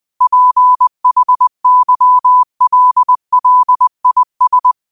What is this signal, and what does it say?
Morse code, heard as a keyed tone, about 20 wpm, 1000 Hz. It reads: PHYLLIS
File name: qotdfiles / 20wpm